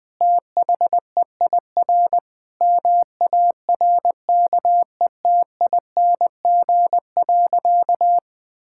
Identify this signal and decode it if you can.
Morse code, heard as a keyed tone, about 20 wpm, 700 Hz. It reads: THEIR MARKETING.
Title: Sentences from Top 1000 Words - Review Part 5 - 20wpm